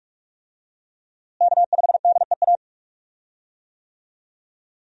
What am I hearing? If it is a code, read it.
Morse code, heard as a keyed tone, about 47 wpm, 700 Hz. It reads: K5DEA
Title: USA Call Signs - 5 Characters - Repeated 2x - 45wpm